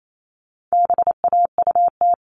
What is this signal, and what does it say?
Morse code, heard as a keyed tone, about 28 wpm, 700 Hz. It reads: BAUT
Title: All Cut Numbers - 4-Digit 28wpm